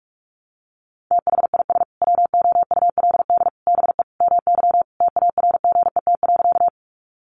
Morse code, heard as a keyed tone, about 44 wpm, 700 Hz. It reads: THIS WOULD BE MY TARGET.
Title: Sentences from Top 1200 Words 45wpm